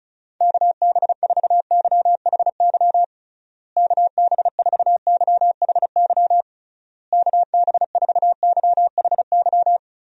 Morse code, heard as a keyed tone, about 34 wpm, 700 Hz. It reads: KB4YHY KB4YHY KB4YHY
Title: USA Call Signs - 6 Characters - Repeated 3x - 35wpm